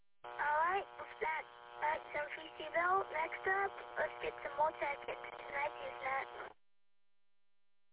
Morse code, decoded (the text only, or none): none